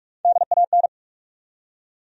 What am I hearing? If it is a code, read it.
Morse code, heard as a keyed tone, about 42 wpm, 700 Hz. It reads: DAN
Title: USA Common Names - Max 5 Letters - 45wpm